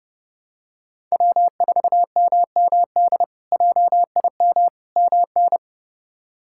Morse code, heard as a keyed tone, about 30 wpm, 700 Hz. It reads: W4MMD JIM MN
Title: K1USN SST QSO Practice - 30wpm